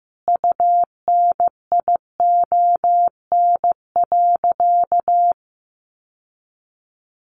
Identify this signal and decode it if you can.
Morse code, heard as a keyed tone, about 15 wpm, 700 Hz. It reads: UNION.
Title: Sentences from Top 1400 Words 15wpm